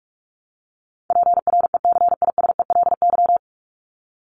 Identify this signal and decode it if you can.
Morse code, heard as a keyed tone, about 44 wpm, 700 Hz. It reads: PRECISELY
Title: LY Suffix - Top 100 Words with Base Words 45wpm